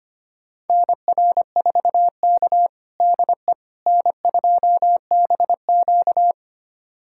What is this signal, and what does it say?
Morse code, heard as a keyed tone, about 25 wpm, 700 Hz. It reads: NR4K DE N2BQ
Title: USA Callsign DE Callsign - 4 Characters - 25wpm